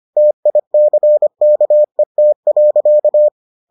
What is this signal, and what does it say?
Morse code, heard as a keyed tone, about 25 wpm, 600 Hz. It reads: TICKET.